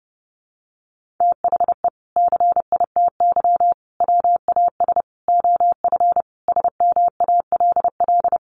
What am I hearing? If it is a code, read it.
Morse code, heard as a keyed tone, about 30 wpm, 700 Hz. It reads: THE CITY WAS OF SMALL
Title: Sentences from Top 1500 Words 30wpm